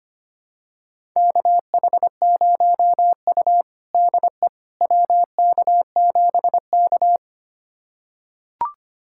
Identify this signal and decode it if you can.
Morse code, heard as a keyed tone, about 25 wpm, 700 Hz. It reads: KH0U DE WK7K
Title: USA Callsign DE Callsign - 4 Characters - 25wpm